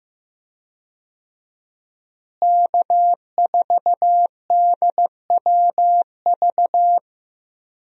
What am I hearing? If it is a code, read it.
Morse code, heard as a keyed tone, about 15 wpm, 700 Hz. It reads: K4DWV
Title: USA Call Signs - 5 Characters - Repeated 2x - 15wpm